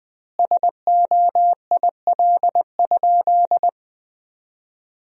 Morse code, heard as a keyed tone, about 21 wpm, 700 Hz. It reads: SOIL?
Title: Sentences from Top 1000 Words - Review Part 4 - 20wpm